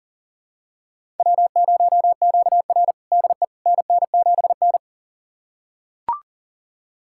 Morse code, heard as a keyed tone, about 39 wpm, 700 Hz. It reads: W0QR DE NN7N
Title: USA Callsign DE Callsign - 4 Characters - 40wpm